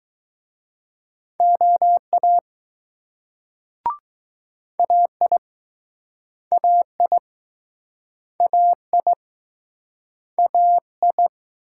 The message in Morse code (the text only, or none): OA AI AI AI AI